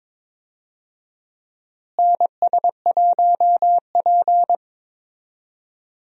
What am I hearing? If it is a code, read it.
Morse code, heard as a keyed tone, about 22 wpm, 700 Hz. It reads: NS1P
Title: USA Call Signs - 4 Characters - Repeated 3x - 22wpm